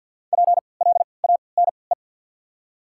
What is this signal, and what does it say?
Morse code, heard as a keyed tone, about 47 wpm, spaced as for 28 wpm, 700 Hz. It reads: PLANE
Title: 3x Character Spacing - Top 500 Words - 50wpm